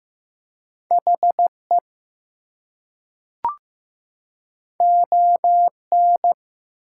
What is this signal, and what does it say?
Morse code, heard as a keyed tone, about 15 wpm, 700 Hz. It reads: HE ON
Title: Two Letter Words - 15wpm